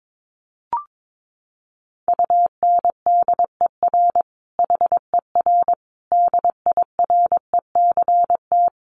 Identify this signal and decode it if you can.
Morse code, heard as a keyed tone, about 22 wpm, 700 Hz. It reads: UNDER HER DIRECT
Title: Sentences from Top 1300 Words 22wpm